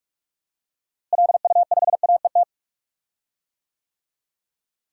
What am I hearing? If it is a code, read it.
Morse code, heard as a keyed tone, about 47 wpm, 700 Hz. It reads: LU5AET
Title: DXCC-100 Call Signs - 6 Characters - Repeated 2x - 45wpm